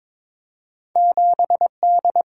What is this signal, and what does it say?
Morse code, heard as a keyed tone, about 22 wpm, 700 Hz. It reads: 7D
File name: practice / 22wpm / Two Letter-Number Combinations - 22wpm